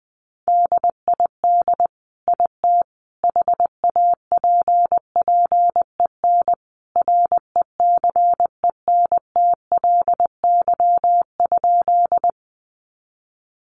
Morse code, heard as a keyed tone, about 20 wpm, 700 Hz. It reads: DID IT HAPPEN RECENTLY?
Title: Sentences from Top 1200 Words 20wpm